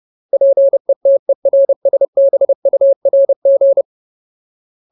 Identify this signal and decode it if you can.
Morse code, heard as a keyed tone, about 30 wpm, 550 Hz. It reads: PETERSBURG
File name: Ultimate Word List / 30WPM Ultimate